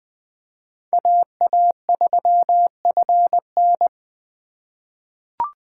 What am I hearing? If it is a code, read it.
Morse code, heard as a keyed tone, about 20 wpm, 700 Hz. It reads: AA3FN
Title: USA Call Signs - 4x Character Spacing - 20wpm